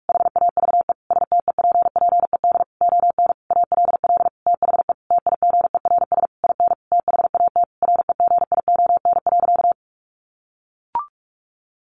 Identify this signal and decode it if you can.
Morse code, heard as a keyed tone, about 44 wpm, 700 Hz. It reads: HAVE STEPPED ON ALL THE TIGERS IN THAT REGION.
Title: A Sound Of Thunder 45wpm